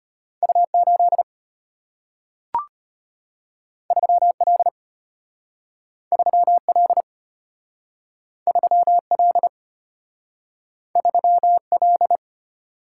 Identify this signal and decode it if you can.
Morse code, heard as a keyed tone, about 33 wpm, 690 Hz. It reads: U8 3L 3L 3L 3L